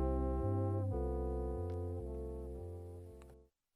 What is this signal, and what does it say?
Dial tone (North America), 350 + 440 Hz, with noise.
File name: wrct-log hi- 06